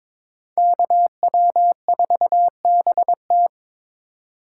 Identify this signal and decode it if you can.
Morse code, heard as a keyed tone, about 26 wpm, 700 Hz. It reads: KW4BT
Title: Speed Racing - USA 2x2 Call Signs 22wpm